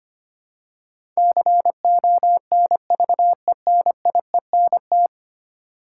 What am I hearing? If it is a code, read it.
Morse code, heard as a keyed tone, about 25 wpm, 700 Hz. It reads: CONVENIENT
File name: Ten Letter Words - 25wpm